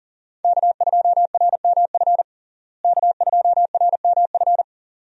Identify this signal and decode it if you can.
Morse code, heard as a keyed tone, about 40 wpm, 700 Hz. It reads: K2RMF K2RMF
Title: USA Call Signs - 5 Characters - Repeated 2x - 40wpm